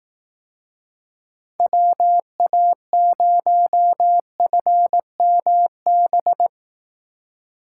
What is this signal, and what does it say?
Morse code, heard as a keyed tone, about 18 wpm, 700 Hz. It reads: WA0FMB
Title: Speed Racing - USA 2x3 Call Signs 15wpm